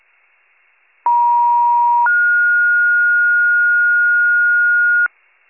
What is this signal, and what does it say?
Two-tone page, heard as a one-second tone, then a three-second tone, about 950 then 1430 Hz, with noise. It is heard from 1.1 to 5.1 s.